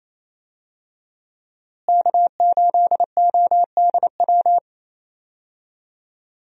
Morse code, heard as a keyed tone, about 27 wpm, 700 Hz. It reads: K8ODW
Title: USA Call Signs - 5 Characters - Repeated 3x - 28wpm